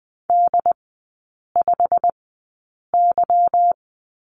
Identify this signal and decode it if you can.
Morse code, heard as a keyed tone, about 20 wpm, spaced as for 8 wpm, 690 Hz. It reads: D5Y
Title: 212 Lesson 34e - Number 8 - Three Characters 20wpm